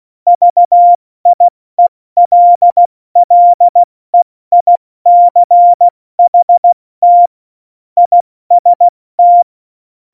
Morse code, heard as a keyed tone, about 16 wpm, 700 Hz. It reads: VIELLEICHT IST
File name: Morsekurs / Witze / Witz 21. August 2018, 70 F80